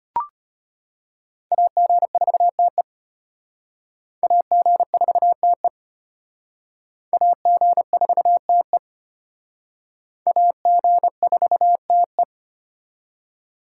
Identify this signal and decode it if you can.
Morse code, heard as a keyed tone, about 33 wpm, 700 Hz. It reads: AG4TE AG4TE AG4TE AG4TE